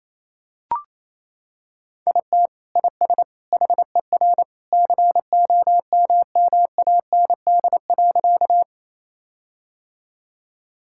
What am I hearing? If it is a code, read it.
Morse code, heard as a keyed tone, about 28 wpm, 700 Hz. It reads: IT IS HER COMMAND.
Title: Sentences from Top 1600 Words 28wpm